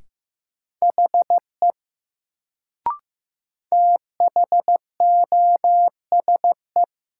Morse code, heard as a keyed tone, about 15 wpm, 700 Hz. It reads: HE THOSE
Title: Sets of 2 Words - New Word Drill - 15wpm